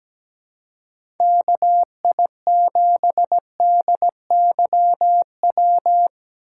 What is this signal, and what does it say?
Morse code, heard as a keyed tone, about 17 wpm, 700 Hz. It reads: KI7DYW
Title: USA Call Signs - 6 Characters - 6x Character Spacing - 17wpm